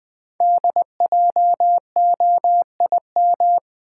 Morse code, heard as a keyed tone, about 20 wpm, 700 Hz. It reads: DJOIM
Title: Code Groups - 5 Letters - 20wpm